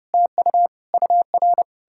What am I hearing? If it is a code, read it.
Morse code, heard as a keyed tone, about 40 wpm, 700 Hz. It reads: TU UR